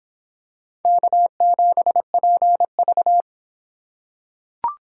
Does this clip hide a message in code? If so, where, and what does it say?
Yes: Morse code from 0.8 to 3.2 s, heard as a keyed tone, about 26 wpm, 700 Hz. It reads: K7PV